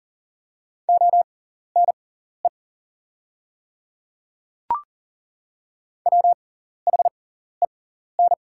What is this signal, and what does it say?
Morse code, heard as a keyed tone, about 40 wpm, spaced as for 14 wpm, 700 Hz. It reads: ONE WHEN
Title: 6x Character Spacing - Top 100 Words - 40wpm